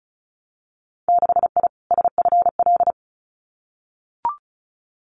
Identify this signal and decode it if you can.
Morse code, heard as a keyed tone, about 36 wpm, 700 Hz. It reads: 6I SFL
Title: Winter Field Day Exchanges Encore 35wpm